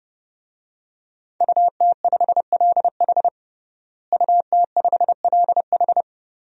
Morse code, heard as a keyed tone, about 32 wpm, 700 Hz. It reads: UT5LH UT5LH
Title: DXCC-100 Call Signs - 5 Characters - Repeated 3x - 30wpm